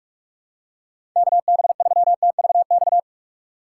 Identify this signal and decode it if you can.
Morse code, heard as a keyed tone, about 45 wpm, 700 Hz. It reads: KB3TVX
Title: USA Call Signs - 3x Character Spacing - 45wpm